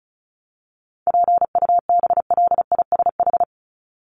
Morse code, heard as a keyed tone, about 34 wpm, 700 Hz. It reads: PUBLISH
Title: ED Suffix - Top 100 Words with Base Words 35wpm